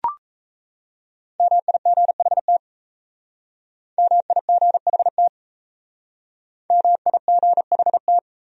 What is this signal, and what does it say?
Morse code, heard as a keyed tone, about 35 wpm, 700 Hz. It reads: MIGHT MIGHT MIGHT